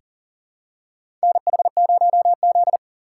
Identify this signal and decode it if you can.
Morse code, heard as a keyed tone, about 40 wpm, 700 Hz. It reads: NH0Z
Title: USA Call Signs - 3x Character Spacing - 40wpm